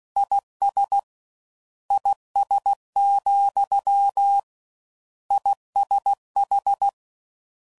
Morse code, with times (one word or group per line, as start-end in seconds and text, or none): 0.2-1.0 IS
1.9-4.4 IS,
5.3-6.9 ISH